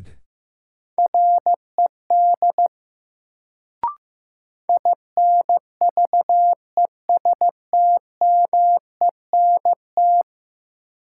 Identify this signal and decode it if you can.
Morse code, heard as a keyed tone, about 15 wpm, 700 Hz. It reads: RED INVESTMENT
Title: Sentences from Top 700 Words - New Word Drill - 15wpm